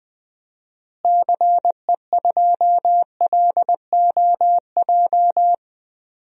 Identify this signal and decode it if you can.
Morse code, heard as a keyed tone, about 20 wpm, 700 Hz. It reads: CE2LOJ
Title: DXCC-100 Call Signs - 6 Characters - Repeated 3x - 20wpm